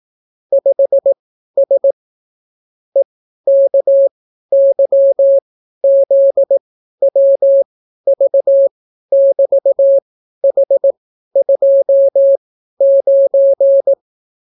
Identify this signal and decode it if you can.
Morse code, heard as a keyed tone, about 18 wpm, spaced as for 12 wpm, 550 Hz. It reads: 5S EKYZWV<BT>H29